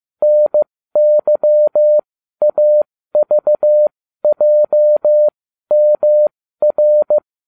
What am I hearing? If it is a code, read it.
Morse code, heard as a keyed tone, about 15 wpm, spaced as for 12 wpm, 600 Hz. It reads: NYAVJMR